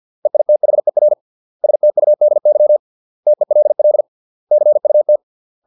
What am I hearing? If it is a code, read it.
Morse code, heard as a keyed tone, about 50 wpm, 600 Hz. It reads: EITHER STUDY TELL CUT